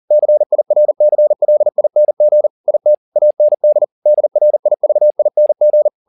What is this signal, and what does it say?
Morse code, heard as a keyed tone, about 40 wpm, 600 Hz. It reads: CIRCLING IT AND DRIVING